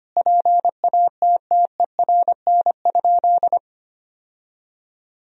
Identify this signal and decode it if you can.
Morse code, heard as a keyed tone, about 25 wpm, 700 Hz. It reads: PATTERN?
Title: Sentences from Top 1000 Words - Review Part 2 - 25wpm